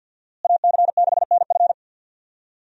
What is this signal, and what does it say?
Morse code, heard as a keyed tone, about 49 wpm, 700 Hz. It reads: AC6NF
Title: USA Call Signs - 5 Characters - Repeated 2x - 50wpm